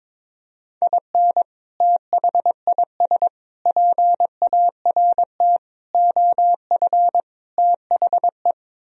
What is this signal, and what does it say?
Morse code, heard as a keyed tone, about 22 wpm, 700 Hz. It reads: IN THIS PART OF THE
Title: Speed-Racing - Sets of 5 Words 22wpm